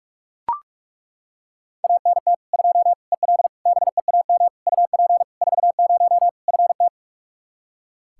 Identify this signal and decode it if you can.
Morse code, heard as a keyed tone, about 41 wpm, 700 Hz. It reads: ANT 2 EL BEAM UP 40 FT
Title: Practice QSO - Exchange 2 - Info - 45wpm